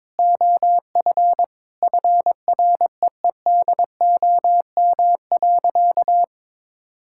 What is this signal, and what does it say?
Morse code, heard as a keyed tone, about 21 wpm, 700 Hz. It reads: OF FREEDOM.
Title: Sentences from Top 1000 Words - Review Part 5 - 22wpm